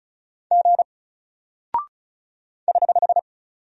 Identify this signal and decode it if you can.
Morse code, heard as a keyed tone, about 36 wpm, 700 Hz. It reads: G <HH>
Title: Single Character (Letter:Number:Punctuation:Prosign) Alt 35wpm